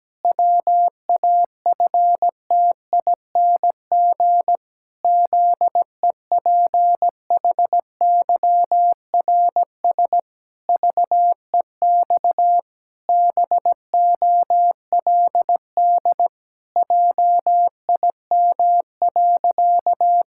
Morse code, heard as a keyed tone, about 17 wpm, 700 Hz. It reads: WAFTING ZEPHYRS VEX BOLD JIM.
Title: Pangram Challenge v2 - 17wpm